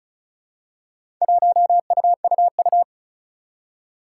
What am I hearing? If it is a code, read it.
Morse code, heard as a keyed tone, about 36 wpm, 700 Hz. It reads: 1UUU
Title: Four Challenging Letter-Number Combinations 35wpm